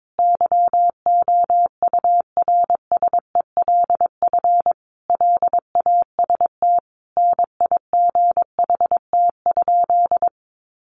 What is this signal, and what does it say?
Morse code, heard as a keyed tone, about 22 wpm, 700 Hz. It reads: YOURSELF LAST NIGHT?
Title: Sentences from Top 600 Words 22wpm